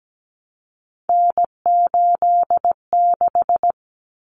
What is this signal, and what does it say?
Morse code, heard as a keyed tone, about 17 wpm, 700 Hz. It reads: N86
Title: Common Cut Numbers - 3-Digit 17wpm